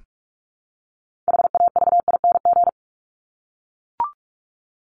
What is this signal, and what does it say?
Morse code, heard as a keyed tone, about 44 wpm, 700 Hz. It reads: HAVING
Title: ING Suffix - Top 100 Words with Base Words 45wpm